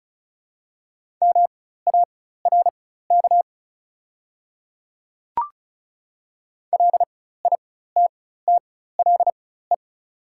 Morse code, heard as a keyed tone, about 36 wpm, spaced as for 16 wpm, 700 Hz. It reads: MARK LITTLE
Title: 4x Character Spacing - Top 300 Words - 35wpm